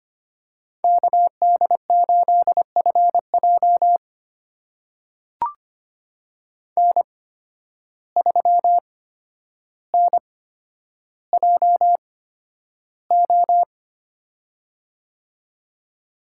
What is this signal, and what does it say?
Morse code, heard as a keyed tone, about 25 wpm, spaced as for 11 wpm, 700 Hz. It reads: KD8FJ N3NJO
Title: USA Call Signs - 5 Characters - 8x Character Spacing - 25wpm